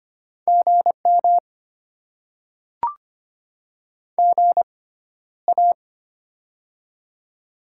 Morse code, heard as a keyed tone, about 25 wpm, spaced as for 13 wpm, 700 Hz. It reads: GM GA
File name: Top 20 QSO Elements - 6x Character Spacing - 25wpm